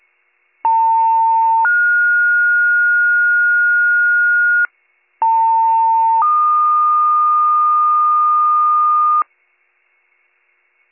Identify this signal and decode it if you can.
Two-tone page, heard as a one-second tone, then a three-second tone, about 900 then 1430 Hz, about 900 then 1220 Hz. It is heard from 0.6 to 9.2 s.